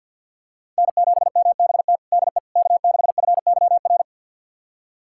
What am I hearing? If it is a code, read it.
Morse code, heard as a keyed tone, about 46 wpm, 690 Hz. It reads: N7MBT DE K6FYR
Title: USA Callsign DE Callsign - 5 Characters - 50wpm